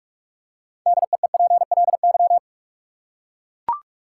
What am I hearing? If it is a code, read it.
Morse code, heard as a keyed tone, about 43 wpm, 700 Hz. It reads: DEEPLY
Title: Six Letter Words - 45wpm